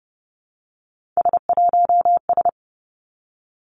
Morse code, heard as a keyed tone, about 30 wpm, 700 Hz. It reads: S1S
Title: Three Challenging Letter-Number Combinations 30wpm